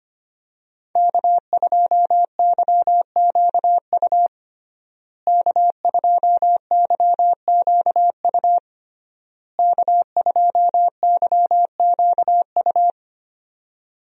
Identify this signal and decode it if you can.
Morse code, heard as a keyed tone, about 25 wpm, 700 Hz. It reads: K2YQU K2YQU K2YQU
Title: USA Call Signs - 5 Characters - Repeated 3x - 25wpm